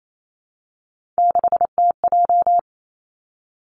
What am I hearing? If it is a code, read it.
Morse code, heard as a keyed tone, about 28 wpm, 700 Hz. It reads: 6TJ
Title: Three Challenging Letter-Number Combinations 28wpm